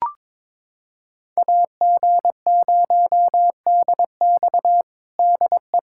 Morse code, heard as a keyed tone, about 22 wpm, 700 Hz. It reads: AG0DX DE
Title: USA Callsign DE Callsign - 5 Characters - 22wpm